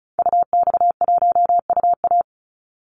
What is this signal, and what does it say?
Morse code, heard as a keyed tone, about 34 wpm, 700 Hz. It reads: UX1UA
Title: Call Signs - 35wpm